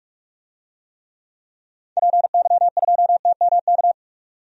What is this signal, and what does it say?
Morse code, heard as a keyed tone, about 45 wpm, 700 Hz. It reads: PY2TMK
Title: DXCC-100 Call Signs - 6 Characters - 45wpm